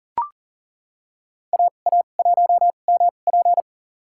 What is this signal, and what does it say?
Morse code, heard as a keyed tone, about 38 wpm, spaced as for 28 wpm, 700 Hz. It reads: AA1MP
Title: USA Call Signs - 5 Characters - 2x Character Spacing - 40wpm